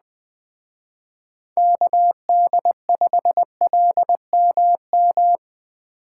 Morse code, heard as a keyed tone, about 20 wpm, 700 Hz. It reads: KD5LMM